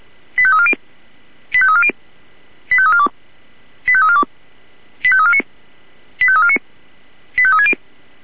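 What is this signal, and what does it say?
Five-tone selective call (ZVEI), heard as quick runs of five tones, tones of about 70 ms. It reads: 85340 85340 85342 85342 85349 85349 85360
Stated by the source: ZVEI five-tone selective call 85369 received on an SDR.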